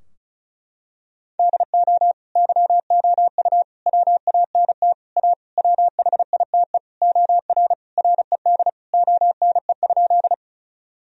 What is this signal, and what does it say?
Morse code, heard as a keyed tone, about 34 wpm, 700 Hz. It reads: DO YOU WANT A WHITE OR RED ONE?
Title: Sentences from Top 1000 Words - Review Part 2 - 35wpm